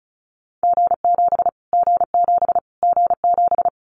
Morse code, heard as a keyed tone, about 34 wpm, 700 Hz. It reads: G7 G7 G7
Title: Callsign Prefixes for Top 25 DXCC - Repeated 3x - 35wpm